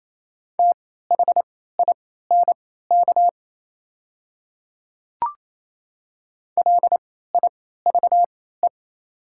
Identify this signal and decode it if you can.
Morse code, heard as a keyed tone, about 28 wpm, spaced as for 16 wpm, 690 Hz. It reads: THINK LIVE
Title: 3x Character Spacing - Top 200 Words - 28wpm